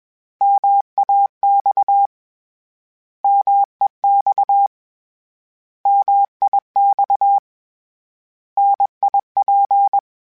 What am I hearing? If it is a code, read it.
Morse code, heard as a keyed tone, about 21 wpm, 800 Hz. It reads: MAX MEX MIX NIP
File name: Contrasts / Contrasts-mp3s / 3LtrContrasts @20